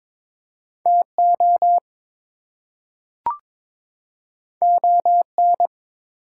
Morse code, heard as a keyed tone, about 22 wpm, 700 Hz. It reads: TO ON
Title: Two Letter Words - 22wpm